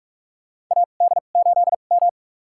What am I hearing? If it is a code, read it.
Morse code, heard as a keyed tone, about 42 wpm, spaced as for 31 wpm, 700 Hz. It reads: AD8M
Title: USA Call Signs - 4 Characters - 2x Character Spacing - 45wpm